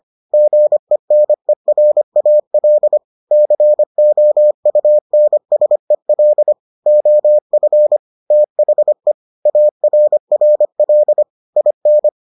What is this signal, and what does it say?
Morse code, heard as a keyed tone, about 25 wpm, 600 Hz. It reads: GENERAL COUNSEL OF THE ARRL IN